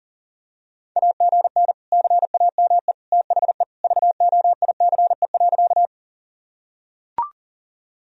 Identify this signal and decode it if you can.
Morse code, heard as a keyed tone, about 40 wpm, 700 Hz. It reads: AGN CAME THE VOICE.
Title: Sentences from Top 500 Words and Abbreviations 40wpm